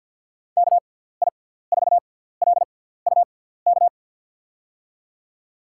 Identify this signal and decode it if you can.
Morse code, heard as a keyed tone, about 52 wpm, spaced as for 17 wpm, 700 Hz. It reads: KI4LUK
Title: USA Call Signs - 6 Characters - 6x Character Spacing - 50wpm